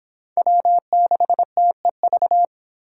Morse code, heard as a keyed tone, about 26 wpm, 700 Hz. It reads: W6TEV